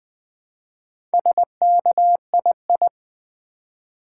Morse code, heard as a keyed tone, about 20 wpm, 700 Hz. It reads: SKII
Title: Build A Word - 2 Letters At A Time - 6-Letter Words - 20wpm